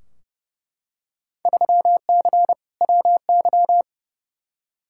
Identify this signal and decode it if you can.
Morse code, heard as a keyed tone, about 32 wpm, 700 Hz. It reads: 3C WY
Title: Field Day Exchanges Encore - 30wpm